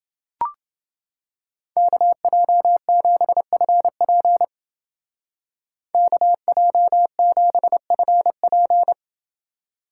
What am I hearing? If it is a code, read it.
Morse code, heard as a keyed tone, about 28 wpm, 700 Hz. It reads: KJ7FP KJ7FP